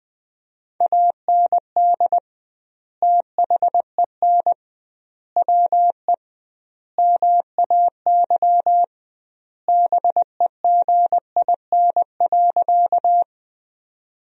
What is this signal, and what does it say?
Morse code, heard as a keyed tone, about 21 wpm, 700 Hz. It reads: AND THEN WE MAY BEGIN.
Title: 2x Word-Spacing - Sentences from Top 300 Words - 20wpm